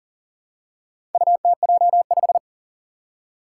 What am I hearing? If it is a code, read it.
Morse code, heard as a keyed tone, about 41 wpm, 700 Hz. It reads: UTJ5
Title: Four Challenging Letter-Number Combinations 40wpm